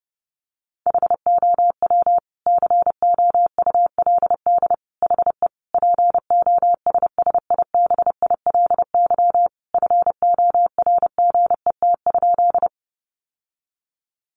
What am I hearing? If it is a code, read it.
Morse code, heard as a keyed tone, about 28 wpm, 700 Hz. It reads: HOW COULD HE POSSIBILY FORGET?
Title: Sentences from Top 1300 Words 30wpm